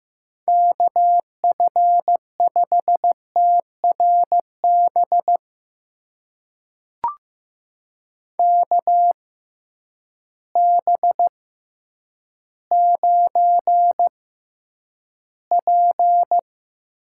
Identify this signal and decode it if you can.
Morse code, heard as a keyed tone, about 15 wpm, spaced as for 9 wpm, 700 Hz. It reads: KF5TRB KB9P
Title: USA Call Signs - 6 Characters - 6x Character Spacing - 15wpm